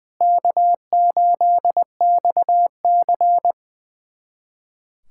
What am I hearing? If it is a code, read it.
Morse code, heard as a keyed tone, about 20 wpm, 690 Hz. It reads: K8XC
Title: USA Callsign DE Callsign 20wpm